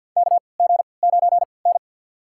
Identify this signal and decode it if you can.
Morse code, heard as a keyed tone, about 53 wpm, spaced as for 29 wpm, 700 Hz. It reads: KG9N
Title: USA Call Signs - 3x Character Spacing - 50wpm